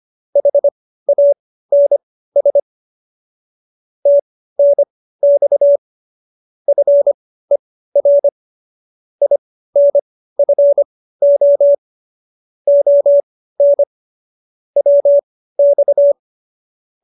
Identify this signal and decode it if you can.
Morse code, heard as a keyed tone, about 25 wpm, spaced as for 15 wpm, 570 Hz. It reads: HANS TNX FER INFO ON WX
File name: qso222 15